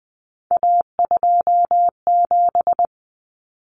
Morse code, heard as a keyed tone, about 20 wpm, 700 Hz. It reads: A27
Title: Common Cut Numbers - 3-Digit 20wpm